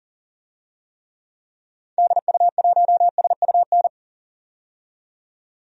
Morse code, heard as a keyed tone, about 40 wpm, 700 Hz. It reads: DU1SUN
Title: DXCC-100 Call Signs - 6 Characters - 40wpm